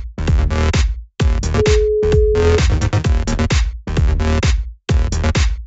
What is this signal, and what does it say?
Ringback tone (Europe, 425 Hz), with noise.